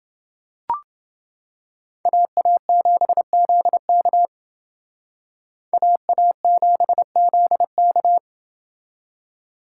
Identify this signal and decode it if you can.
Morse code, heard as a keyed tone, about 28 wpm, 700 Hz. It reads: AA7ZK AA7ZK